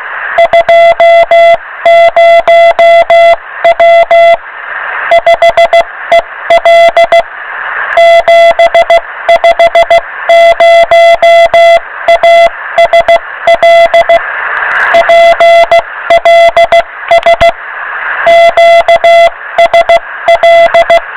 Morse code, heard as a keyed tone, about 16 wpm, 680 Hz, with noise. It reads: 20W 5EL 750ASL PLS QSL